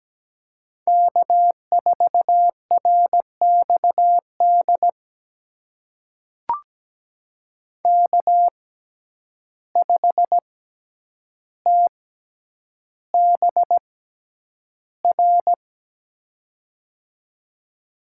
Morse code, heard as a keyed tone, about 17 wpm, spaced as for 9 wpm, 700 Hz. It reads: K4RXD K5TBR
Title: USA Call Signs - 5 Characters - 6x Character Spacing - 17wpm